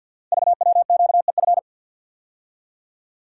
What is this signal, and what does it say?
Morse code, heard as a keyed tone, about 51 wpm, 700 Hz. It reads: VWQEF
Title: Code Groups - 5 Letters - 50wpm